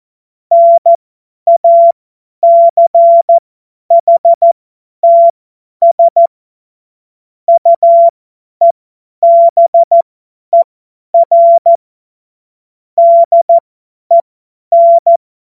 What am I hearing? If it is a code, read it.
Morse code, heard as a keyed tone, about 14 wpm, spaced as for 10 wpm, 680 Hz. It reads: NACHTS UEBER DEN
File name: Morsekurs / Witze / Witz 01. Juni 2021, 50 F70